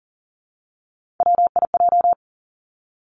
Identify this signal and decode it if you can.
Morse code, heard as a keyed tone, about 38 wpm, 700 Hz. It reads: WIJ
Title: Three Challenging Letter-Number Combinations 40wpm